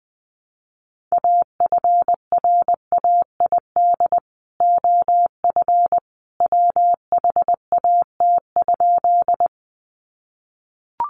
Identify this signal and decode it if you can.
Morse code, heard as a keyed tone, about 20 wpm, 700 Hz. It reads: AFRAID OF WHAT?
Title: Sentences from Top 1500 Words 20wpm